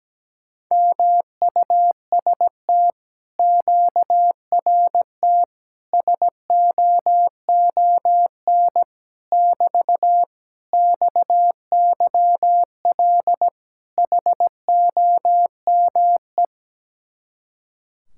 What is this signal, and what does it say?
Morse code, heard as a keyed tone, about 17 wpm, 700 Hz. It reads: MUST QRT SOON <BT> XYL HOME
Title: Speed-Racing Common QSO Phrases - 17wpm